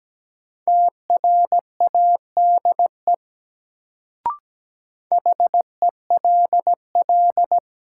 Morse code, heard as a keyed tone, about 17 wpm, 700 Hz. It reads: TRADE HELL
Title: Sets of 5 Words Encore - New Word Drill - 17wpm